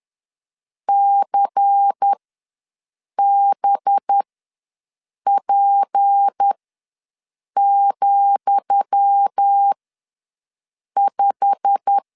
Morse code, heard as a keyed tone, about 10 wpm, spaced as for 6 wpm, 800 Hz. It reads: CBP,5
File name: LEZIONI / 10 SIGNS / 10D COMA ws 10wpm 4314